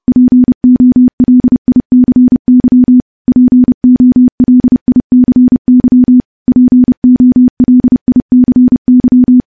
Morse code, heard as a keyed tone, about 30 wpm, 250 Hz. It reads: POLICY POLICY POLICY
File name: morse / morse-training-30WPM